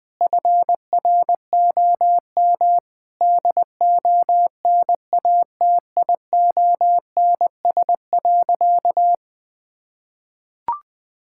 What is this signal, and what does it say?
Morse code, heard as a keyed tone, about 20 wpm, 700 Hz. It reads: FROM DONATIONS.
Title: Sentences with 9-Word Phrases - 20wpm